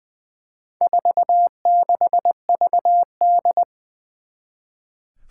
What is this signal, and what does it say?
Morse code, heard as a keyed tone, about 20 wpm, 700 Hz. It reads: 46VD
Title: Four Challenging Letter-Number Combinations 20wpm